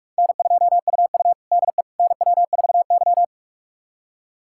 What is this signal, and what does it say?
Morse code, heard as a keyed tone, about 48 wpm, 700 Hz. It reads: N2UU DE NW4Y
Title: USA Callsign DE Callsign - 4 Characters - 45wpm